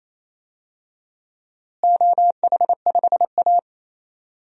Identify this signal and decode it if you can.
Morse code, heard as a keyed tone, about 28 wpm, 700 Hz. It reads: OH5A
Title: DXCC-100 Call Signs - 4 Characters - Repeated 3x - 28wpm